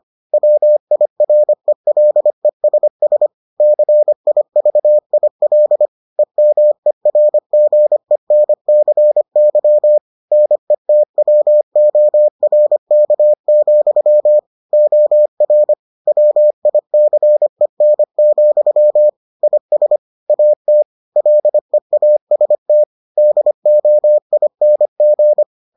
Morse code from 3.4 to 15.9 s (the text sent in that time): CIVIL EMERGENCY NETWORK, OR